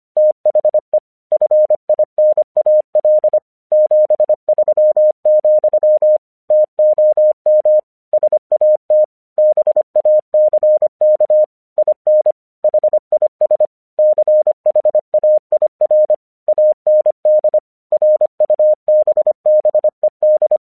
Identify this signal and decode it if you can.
Morse code, heard as a keyed tone, about 25 wpm, 610 Hz. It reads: THE FINAL 73, TOM SAT BACK IN HIS CHAIR AND RUBBED
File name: 1 English 25wpm / AQC32